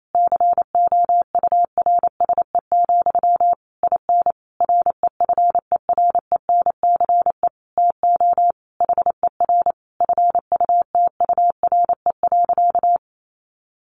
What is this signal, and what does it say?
Morse code, heard as a keyed tone, about 28 wpm, 700 Hz. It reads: COURSE, IN REFERENCE TO HER FUTURE.
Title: Sentences from Top 1200 Words 28wpm